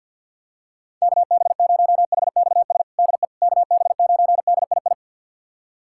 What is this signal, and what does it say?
Morse code, heard as a keyed tone, about 46 wpm, 690 Hz. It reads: KB0HXS DE KB9DII
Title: USA Callsign DE Callsign - 6 Characters - 50wpm